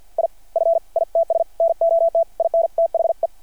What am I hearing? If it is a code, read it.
Morse code, heard as a keyed tone, about 52 wpm, spaced as for 42 wpm, 660 Hz. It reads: IF ITS NOT IN THE
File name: archived / CRSnet